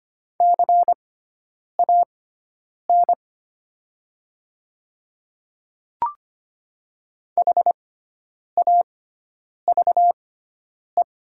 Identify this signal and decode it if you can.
Morse code, heard as a keyed tone, about 25 wpm, spaced as for 9 wpm, 700 Hz. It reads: CAN HAVE
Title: 6x Character Spacing - Top 100 Words - 25wpm